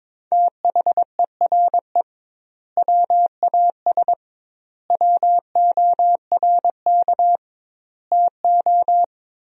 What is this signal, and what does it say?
Morse code, heard as a keyed tone, about 22 wpm, 700 Hz. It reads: THERE WAS WORK TO
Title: 2x Word-Spacing - Sentences from Top 200 Words - 22wpm